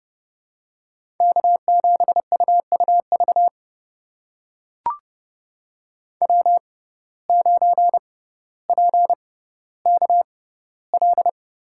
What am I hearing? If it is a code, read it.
Morse code, heard as a keyed tone, about 30 wpm, spaced as for 15 wpm, 700 Hz. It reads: K7UUV W9PKL
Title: USA Call Signs - 6x Character Spacing - 30wpm